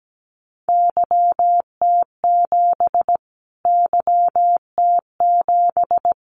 Morse code, heard as a keyed tone, about 17 wpm, 700 Hz. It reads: YT7 YT7
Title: Callsign Prefixes for Top 25 DXCC - Repeated 3x - 17wpm